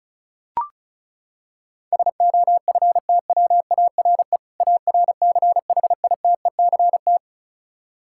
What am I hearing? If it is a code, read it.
Morse code, heard as a keyed tone, about 36 wpm, 700 Hz. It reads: SOFTWARE ARCHITECT
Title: Practice QSO - Exchange 3 - More Info - 35wpm